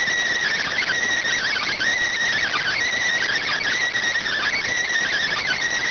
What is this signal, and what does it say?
Emergency Alert System (SAME) tones (header bursts), heard as harsh buzzing bursts. It is heard from 0.3 to 5.1 s.